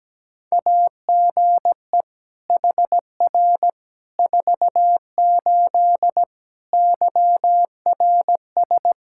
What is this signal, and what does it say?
Morse code, heard as a keyed tone, about 17 wpm, 700 Hz. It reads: AGE HR 48 YRS
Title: Practice QSO - Exchange 3 - More Info - 17wpm